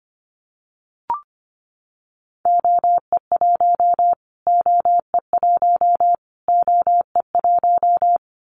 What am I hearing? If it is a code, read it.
Morse code, heard as a keyed tone, about 25 wpm, 700 Hz. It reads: OE1 OE1 OE1
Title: Callsign Prefixes for Top 25 DXCC - Repeated 3x - 25wpm